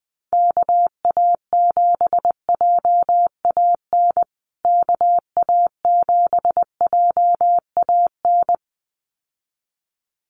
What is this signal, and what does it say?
Morse code, heard as a keyed tone, about 20 wpm, 700 Hz. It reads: KA7JAN KA7JAN
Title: USA Call Signs Repeated 2x 20wpm